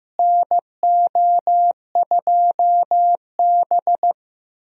Morse code, heard as a keyed tone, about 15 wpm, 700 Hz. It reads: NO2B
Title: USA Call Signs - 4 Characters - Repeated 3x - 15wpm